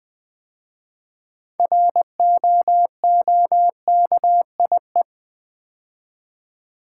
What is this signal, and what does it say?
Morse code, heard as a keyed tone, about 20 wpm, 700 Hz. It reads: ROOKIE
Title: 6 Letter Words - First 3 Letters Given - 20wpm